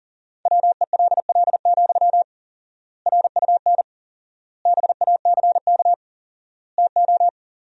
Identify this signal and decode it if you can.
Morse code, heard as a keyed tone, about 37 wpm, 700 Hz. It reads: WELL, RUN BACK TO
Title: 4x Word-Spacing - Sentences from Top 300 Words - 40wpm